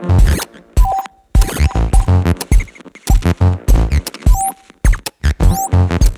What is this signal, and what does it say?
There is no code.